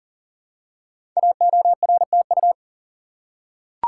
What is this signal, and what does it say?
Morse code, heard as a keyed tone, about 40 wpm, 690 Hz. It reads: AORTU
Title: Code Groups - 5 Letters - 40wpm